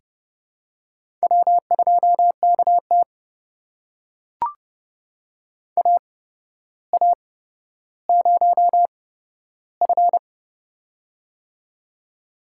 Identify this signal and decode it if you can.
Morse code, heard as a keyed tone, about 30 wpm, spaced as for 13 wpm, 700 Hz. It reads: W2KT AA0F
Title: USA Call Signs - 4 Characters - 8x Character Spacing - 30wpm